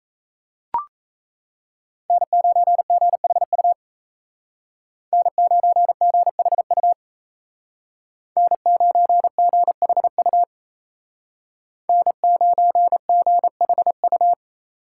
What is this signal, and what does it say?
Morse code, heard as a keyed tone, about 37 wpm, 700 Hz. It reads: N9GHU N9GHU N9GHU N9GHU